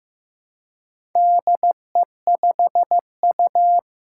Morse code, heard as a keyed tone, about 15 wpm, 700 Hz. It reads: DE5U
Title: Four Challenging Letter-Number Combinations 15wpm